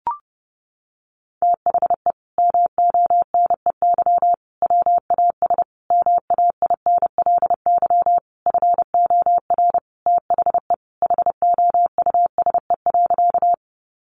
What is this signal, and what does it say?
Morse code, heard as a keyed tone, about 30 wpm, 700 Hz. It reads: THE MONEY WAS MAINLY FOR THE HOUSE.
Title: Sentences from Top 1400 Words 30wpm